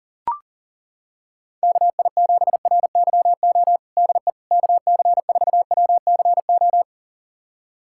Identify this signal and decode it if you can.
Morse code, heard as a keyed tone, about 37 wpm, 700 Hz. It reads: KI7RYO DE KC4WCO
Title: USA Callsign DE Callsign - 6 Characters - 40wpm